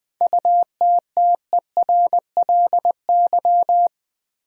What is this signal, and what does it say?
Morse code, heard as a keyed tone, about 20 wpm, 700 Hz. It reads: UTTERLY
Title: 7 Letter Words - First 3 Letters Given - 20wpm